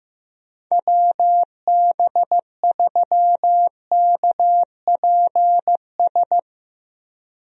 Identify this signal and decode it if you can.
Morse code, heard as a keyed tone, about 15 wpm, 700 Hz. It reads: WB3KPS
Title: USA Call Signs - 6 Characters - Repeated 2x - 15wpm